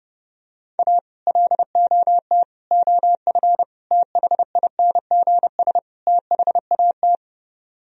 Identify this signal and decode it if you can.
Morse code, heard as a keyed tone, about 30 wpm, 700 Hz. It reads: A LOT OF THINGS THAT